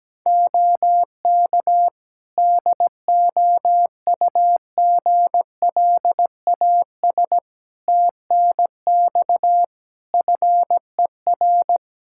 Morse code, heard as a keyed tone, about 17 wpm, 700 Hz. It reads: OK DOUGLAS TNX FER
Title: Practice QSO - Exchange 4 - Ending - 17wpm